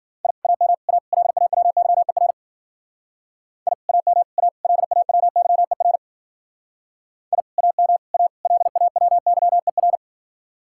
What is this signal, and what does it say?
Morse code, heard as a keyed tone, about 50 wpm, 700 Hz. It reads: I AM A LAWYER I AM A LAWYER I AM A LAWYER